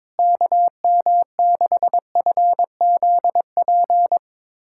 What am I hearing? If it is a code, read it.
Morse code, heard as a keyed tone, about 22 wpm, 700 Hz. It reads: KM6FZP
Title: USA Callsign DE Callsign - 6 Characters - 22wpm